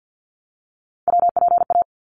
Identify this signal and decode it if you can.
Morse code, heard as a keyed tone, about 50 wpm, 700 Hz. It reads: WPA
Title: ARRL Sections 50wpm